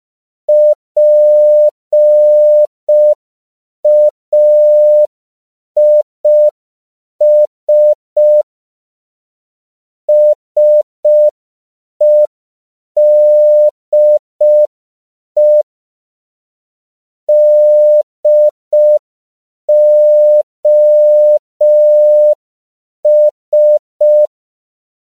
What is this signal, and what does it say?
Morse code, heard as a keyed tone, about 5 wpm, 600 Hz. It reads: PAIS SEDE DOS
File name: CW / 5PPM